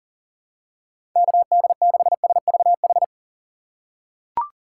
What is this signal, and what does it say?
Morse code, heard as a keyed tone, about 40 wpm, 700 Hz. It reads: KD6SVH
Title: USA Call Signs - 6 Characters - 3x Character Spacing - 40wpm